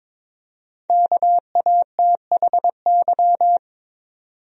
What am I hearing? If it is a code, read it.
Morse code, heard as a keyed tone, about 22 wpm, 700 Hz. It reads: KATHY
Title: USA Common Names - Max 5 Letters - 22wpm